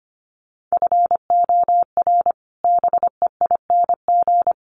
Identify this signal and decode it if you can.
Morse code, heard as a keyed tone, about 25 wpm, 700 Hz. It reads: FOR BEING
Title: Sets of 5 Words Encore2 25wpm